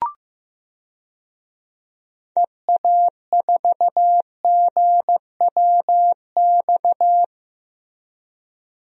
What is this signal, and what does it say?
Morse code, heard as a keyed tone, about 15 wpm, 700 Hz. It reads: EA4GWX
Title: DXCC-100 Call Signs - 6 Characters - Repeated 2x - 15wpm